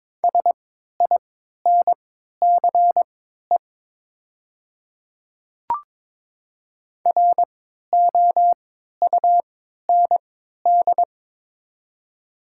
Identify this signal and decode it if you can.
Morse code, heard as a keyed tone, about 22 wpm, spaced as for 12 wpm, 690 Hz. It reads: SINCE ROUND
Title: 3x Character Spacing - Top 300 Words - 22wpm